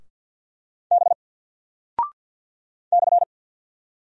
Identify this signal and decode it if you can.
Morse code, heard as a keyed tone, about 48 wpm, 700 Hz. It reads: B /
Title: Single Character (Letter:Number:Punctuation:Prosign) Alt 50wpm